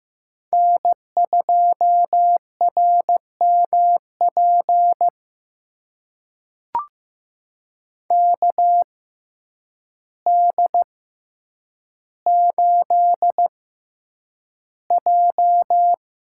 Morse code, heard as a keyed tone, about 15 wpm, spaced as for 8 wpm, 700 Hz. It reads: N2RMP KD8J
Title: USA Call Signs - 6x Character Spacing - 15wpm